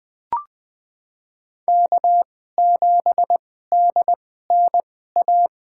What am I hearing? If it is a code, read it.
Morse code, heard as a keyed tone, about 20 wpm, spaced as for 14 wpm, 700 Hz. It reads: K7DNA
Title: USA Call Signs - 5 Characters - 2x Character Spacing - 20wpm